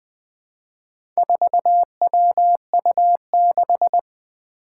Morse code, heard as a keyed tone, about 20 wpm, 700 Hz. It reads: 4WU6
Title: Four Challenging Letter-Number Combinations 20wpm